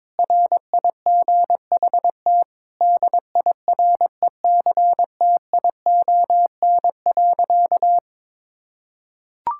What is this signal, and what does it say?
Morse code, heard as a keyed tone, about 22 wpm, 700 Hz. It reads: RIGHT DIRECTION.
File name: Sentences with 7-Word Phrases - 22wpm